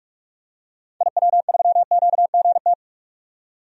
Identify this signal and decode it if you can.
Morse code, heard as a keyed tone, about 46 wpm, 700 Hz. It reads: IW3QGT
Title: DXCC-100 Call Signs - 6 Characters - 45wpm